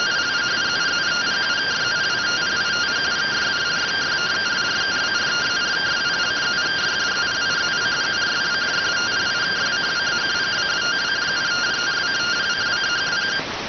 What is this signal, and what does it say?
Two-tone teleprinter signal, about 170 Hz shift.